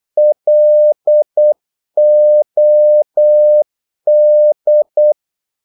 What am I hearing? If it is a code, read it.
Morse code, heard as a keyed tone, about 8 wpm, 600 Hz. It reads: LOD
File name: Korte ord DK 8wpm